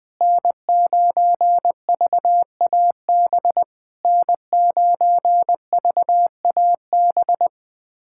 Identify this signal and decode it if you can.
Morse code, heard as a keyed tone, about 20 wpm, 690 Hz. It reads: N9VAB N9VAB
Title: SOTA QSO Practice 20wpm